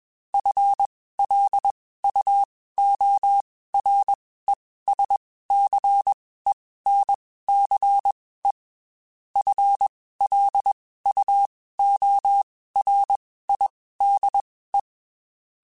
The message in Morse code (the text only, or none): FLUORESCENCE FLUORIDE